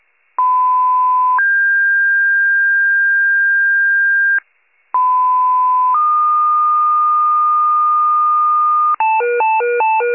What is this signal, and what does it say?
Two-tone page, heard as a one-second tone, then a three-second tone, about 1010 then 1600 Hz, about 1010 then 1220 Hz, with noise. It is heard from 0.4 to 9.0 s.
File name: pager / 94 RESCUE / wcalert